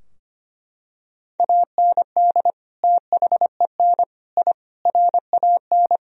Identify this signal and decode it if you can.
Morse code, heard as a keyed tone, about 25 wpm, 700 Hz. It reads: AND THEN I RAN
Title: Sentences from Top 1000 Words - Review Part 3 - 25wpm